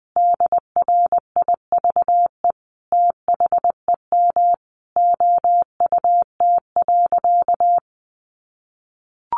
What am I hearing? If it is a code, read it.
Morse code, heard as a keyed tone, about 20 wpm, 700 Hz. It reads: DRIVE THEM OUT.
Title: Sentences from Top 500 Words 20wpm